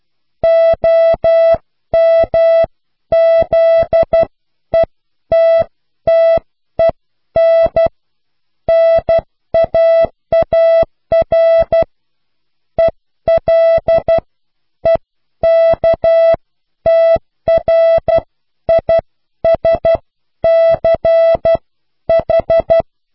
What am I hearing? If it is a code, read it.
Morse code, heard as a keyed tone, about 12 wpm, spaced as for 10 wpm, 660 Hz. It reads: OMZETTEN NAAR ELEKTRISCH